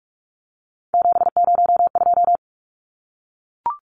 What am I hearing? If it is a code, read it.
Morse code, heard as a keyed tone, about 44 wpm, 700 Hz. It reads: 702
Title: Numbers - 3-Digit 45wpm